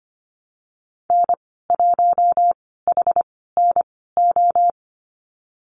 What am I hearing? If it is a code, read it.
Morse code, heard as a keyed tone, about 25 wpm, spaced as for 16 wpm, 690 Hz. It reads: N1HNO
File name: Lesson 15g - Number 5 - Call Signs 25wpm